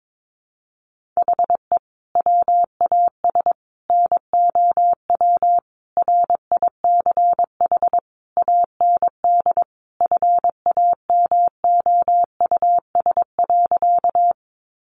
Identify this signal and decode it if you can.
Morse code, heard as a keyed tone, about 22 wpm, 700 Hz. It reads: HE WAS NOW RICH AND FAMOUS.
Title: Sentences from Top 1500 Words 22wpm